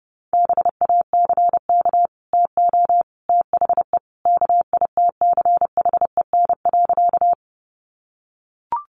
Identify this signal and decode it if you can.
Morse code, heard as a keyed tone, about 29 wpm, 700 Hz. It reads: BACK TO THE KITCHEN.
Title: Sentences from Top 1400 Words 30wpm